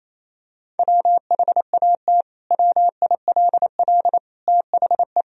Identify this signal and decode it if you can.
Morse code, heard as a keyed tone, about 28 wpm, 700 Hz. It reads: WHAT WILL THE
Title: Sentences from Top 1000 Words - Review Part 3 - 28wpm